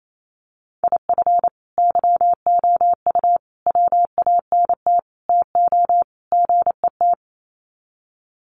Morse code, heard as a keyed tone, about 28 wpm, 700 Hz. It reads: IF YOU WANT TO GET
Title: Sets of 5 Words Encore 28wpm